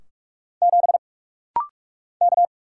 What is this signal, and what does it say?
Morse code, heard as a keyed tone, about 45 wpm, 700 Hz. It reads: 7 K
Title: Single Letter-Number - Rapid-Fire 45wpm